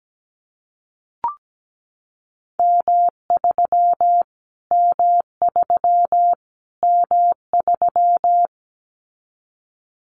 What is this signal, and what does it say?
Morse code, heard as a keyed tone, about 17 wpm, 700 Hz. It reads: M3 M3 M3
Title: Callsign Prefixes for Top 25 DXCC - Repeated 3x - 17wpm